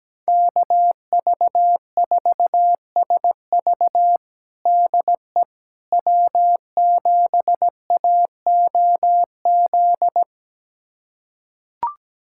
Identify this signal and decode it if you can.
Morse code, heard as a keyed tone, about 17 wpm, 700 Hz. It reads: KV4SV DE W7AOZ
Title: USA Callsign DE Callsign - 5 Characters - 17wpm